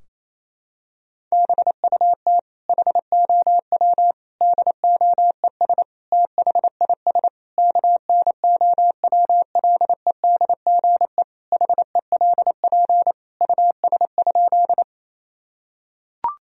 Morse code, heard as a keyed tone, about 27 wpm, 700 Hz. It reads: BUT HOW DOES THIS KNOWLEDGE HELP US?
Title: Sentences from Top 1000 Words - Review Part 3 - 28wpm